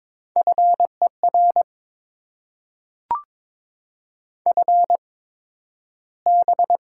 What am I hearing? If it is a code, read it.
Morse code, heard as a keyed tone, about 22 wpm, spaced as for 12 wpm, 700 Hz. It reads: FER FB
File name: Top 20 QSO Elements - 8x Character Spacing - 22wpm